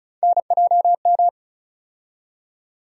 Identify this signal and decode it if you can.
Morse code, heard as a keyed tone, about 36 wpm, 700 Hz. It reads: NJM
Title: USA Call Sign Postfixes - 3 Letters 35wpm